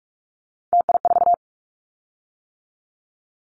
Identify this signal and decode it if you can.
Morse code, heard as a keyed tone, about 46 wpm, 700 Hz. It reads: TI4
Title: Callsign Prefixes for Top 100 DXCC - 45wpm